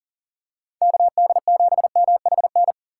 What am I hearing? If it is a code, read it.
Morse code, heard as a keyed tone, about 37 wpm, 700 Hz. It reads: KD7MHN
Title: USA Call Signs - 6 Characters - 3x Character Spacing - 40wpm